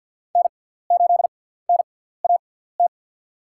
Morse code, heard as a keyed tone, about 52 wpm, spaced as for 17 wpm, 700 Hz. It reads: N8NAT
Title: USA Call Signs - 5 Characters - 6x Character Spacing - 50wpm